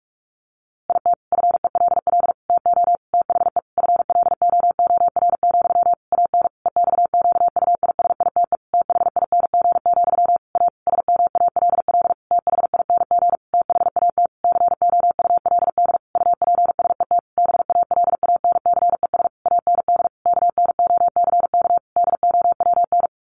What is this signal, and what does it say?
Morse code, heard as a keyed tone, about 44 wpm, 700 Hz. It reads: IT FELL TO THE FLOOR, AN EXQUISITE THING, A SMALL THING THAT COULD UPSET BALANCES AND KNOCK DOWN
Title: A Sound Of Thunder 45wpm